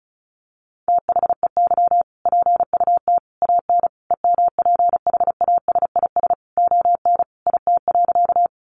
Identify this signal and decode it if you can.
Morse code, heard as a keyed tone, about 34 wpm, 700 Hz. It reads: THEY PUT AN EMPHASIS ON IT.
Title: Sentences from Top 1400 Words 35wpm